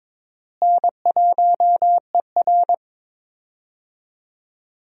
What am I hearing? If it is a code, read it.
Morse code, heard as a keyed tone, about 22 wpm, 700 Hz. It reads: N1ER
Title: USA Call Signs - 4 Characters - Repeated 3x - 22wpm